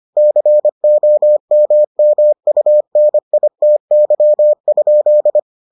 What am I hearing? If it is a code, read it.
Morse code, heard as a keyed tone, about 25 wpm, 600 Hz. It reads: COMMUNITY?